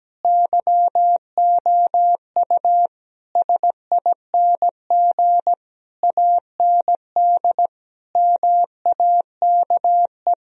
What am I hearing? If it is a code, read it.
Morse code, heard as a keyed tone, about 17 wpm, 700 Hz. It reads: YOU SING AND MAKE
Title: Sentences from Top 1000 Words - Review Part 4 - 17wpm